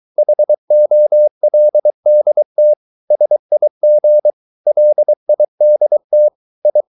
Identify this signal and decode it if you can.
Morse code, heard as a keyed tone, about 23 wpm, 600 Hz. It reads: HOLDT SIG LIDT I
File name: Ferie QRS 23wpm